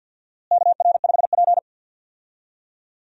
Morse code, heard as a keyed tone, about 48 wpm, 700 Hz. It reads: KR5P
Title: USA Call Signs - 4 Characters - Repeated 3x - 50wpm